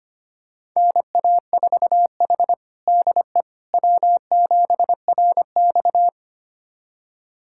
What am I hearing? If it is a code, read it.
Morse code, heard as a keyed tone, about 25 wpm, 700 Hz. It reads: NA4H DE W7RX
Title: USA Callsign DE Callsign - 4 Characters - 25wpm